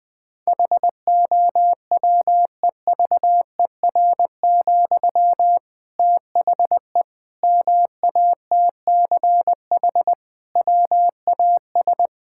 Morse code, heard as a keyed tone, about 21 wpm, 700 Hz. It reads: HOWEVER, THE MATCH WAS
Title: Sentences from Top 1600 Words 20wpm